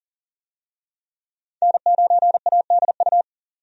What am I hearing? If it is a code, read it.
Morse code, heard as a keyed tone, about 37 wpm, 700 Hz. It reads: N9ADU
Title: USA Call Signs - 5 Characters - 40wpm